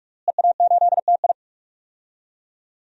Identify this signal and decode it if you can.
Morse code, heard as a keyed tone, about 46 wpm, 700 Hz. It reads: EA8TI
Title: DXCC-100 Call Signs - 5 Characters - Repeated 2x - 45wpm